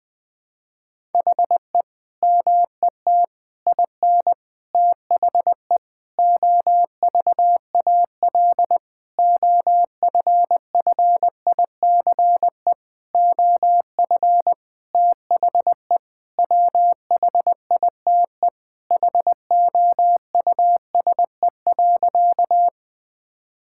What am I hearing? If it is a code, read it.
Morse code, heard as a keyed tone, about 20 wpm, 700 Hz. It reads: HE MET IN THE OVAL OFFICE OF THE WHITE HOUSE.
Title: Sentences with 8-Word Phrases - 20wpm